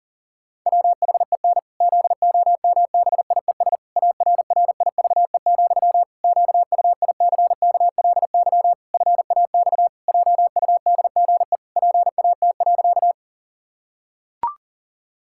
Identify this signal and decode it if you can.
Morse code, heard as a keyed tone, about 40 wpm, 700 Hz. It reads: WHEN ZOMBIES ARRIVE, QUICKLY FAX JUDGE PAT.
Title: Pangram Challenge v2 - 40wpm